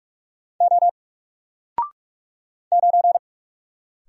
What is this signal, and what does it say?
Morse code, heard as a keyed tone, about 44 wpm, 700 Hz. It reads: O 9
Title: Single Letter-Number 45wpm